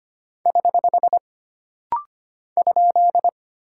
Morse code, heard as a keyed tone, about 25 wpm, 700 Hz. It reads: <HH> ?
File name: Single Punctuation-Prosign - Rapid-Fire - 25wpm